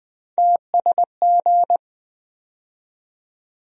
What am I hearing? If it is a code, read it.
Morse code, heard as a keyed tone, about 20 wpm, 700 Hz. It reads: TSG
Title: USA Call Sign Postfixes - 3 Letters 20wpm